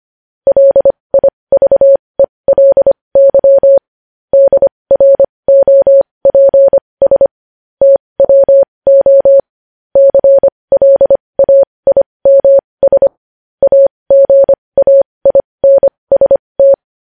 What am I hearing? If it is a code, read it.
Morse code, heard as a keyed tone, about 25 wpm, spaced as for 20 wpm, 550 Hz. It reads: LIVELY DROPS TWO CLAIMS AGAINST
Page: news-headlines-in-morse-code-at-20-wpm